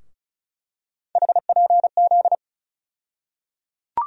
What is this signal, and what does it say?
Morse code, heard as a keyed tone, about 34 wpm, 700 Hz. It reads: HPZ
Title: USA Call Sign Postfixes - 3 Letters 35wpm